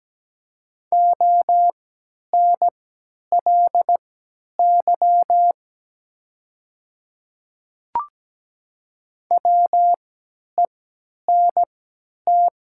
Morse code, heard as a keyed tone, about 17 wpm, spaced as for 10 wpm, 700 Hz. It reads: ONLY WENT
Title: 3x Character Spacing - Top 300 Words - 17wpm